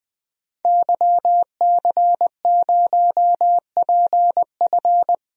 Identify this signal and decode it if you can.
Morse code, heard as a keyed tone, about 20 wpm, 700 Hz. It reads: YC0PF
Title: DXCC-100 Call Signs - 5 Characters - Repeated 2x - 20wpm